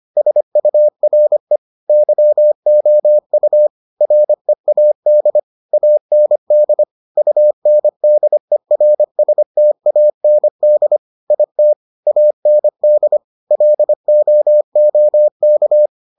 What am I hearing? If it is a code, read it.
Morse code, heard as a keyed tone, about 25 wpm, 600 Hz. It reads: SURE YOU READ AND UNDERSTAND IT AND LOOK